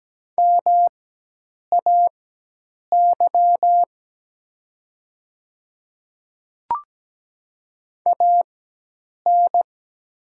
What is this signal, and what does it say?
Morse code, heard as a keyed tone, about 17 wpm, spaced as for 8 wpm, 700 Hz. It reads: MAY AN
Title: 4x Character Spacing - Top 100 Words - 17wpm